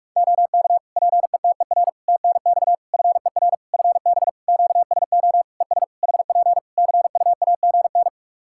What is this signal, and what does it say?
Morse code, heard as a keyed tone, about 48 wpm, 700 Hz. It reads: OK PETER TNX FER FB QSO ES HP CUAGN
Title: K1USN SST QSO Practice 45wpm